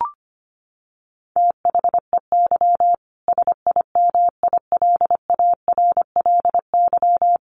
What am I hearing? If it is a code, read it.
Morse code, heard as a keyed tone, about 25 wpm, 700 Hz. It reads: THEY SIMILARLY
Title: Sentences from Top 1400 Words 25wpm